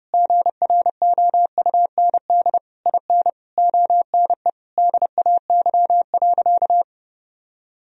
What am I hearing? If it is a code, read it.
Morse code, heard as a keyed tone, about 30 wpm, 700 Hz. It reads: GROUND IN ONE DAY.
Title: Sentences from Top 1000 Words - Review Part 3 - 30wpm